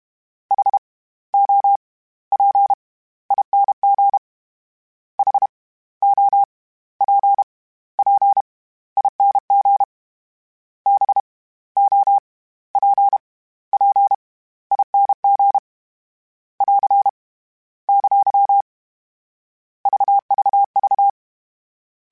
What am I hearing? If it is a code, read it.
Morse code, heard as a keyed tone, about 32 wpm, 800 Hz. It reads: H O P ING H O P P ING B O P P ING <AR> ! VVV